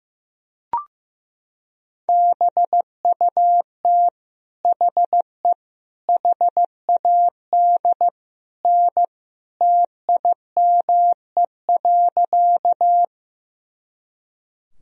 Morse code, heard as a keyed tone, about 15 wpm, 700 Hz. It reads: BUT HE HAD N TIME.
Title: Sentences from Top 100 Words and Abbreviations 15wpm